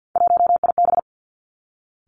Morse code, heard as a keyed tone, about 50 wpm, 700 Hz. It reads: 1IB
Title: Three Challenging Letter-Number Combinations 50wpm